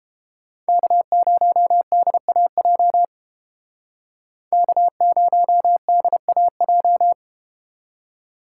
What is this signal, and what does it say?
Morse code, heard as a keyed tone, about 30 wpm, 700 Hz. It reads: K0DAJ K0DAJ